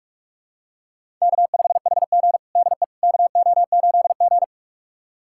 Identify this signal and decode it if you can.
Morse code, heard as a keyed tone, about 47 wpm, 700 Hz. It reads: K5HG DE KO8G
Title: USA Callsign DE Callsign - 4 Characters - 45wpm